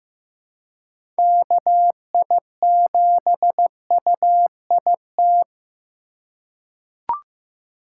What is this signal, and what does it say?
Morse code, heard as a keyed tone, about 15 wpm, 700 Hz. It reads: KI7UIT
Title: USA Call Signs - 6 Characters - 4x Character Spacing - 15wpm